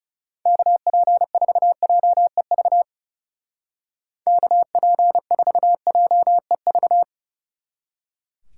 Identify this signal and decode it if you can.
Morse code, heard as a keyed tone, about 37 wpm, 700 Hz. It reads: KP4JEV KP4JEV
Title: Speed Racing - USA 2x3 Call Signs 30wpm